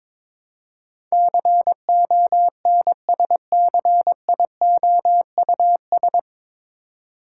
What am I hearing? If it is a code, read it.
Morse code, heard as a keyed tone, about 21 wpm, 700 Hz. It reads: CONSCIOUS
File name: 9 Letter Words - First 3 Letters Given - 22wpm